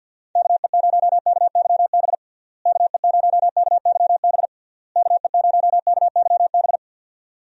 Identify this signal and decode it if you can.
Morse code, heard as a keyed tone, about 49 wpm, 700 Hz. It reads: KE0KYB KE0KYB KE0KYB
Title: USA Call Signs - 6 Characters - Repeated 3x - 50wpm